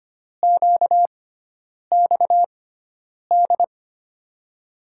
Morse code, heard as a keyed tone, about 25 wpm, spaced as for 9 wpm, 700 Hz. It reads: QXD
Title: USA Call Signs - 5 Characters - 6x Character Spacing - 25wpm